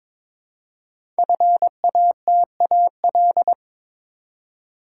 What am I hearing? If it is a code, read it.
Morse code, heard as a keyed tone, about 22 wpm, 700 Hz. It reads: FATAL
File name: 5 Letter Words - First 3 Letters Given - 22wpm